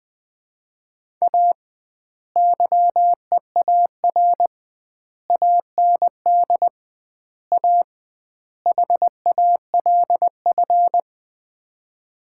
Morse code, heard as a keyed tone, about 20 wpm, 700 Hz. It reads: A YEAR AND A HALF
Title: 2x Word-Spacing - Sets of 5 Words - 20wpm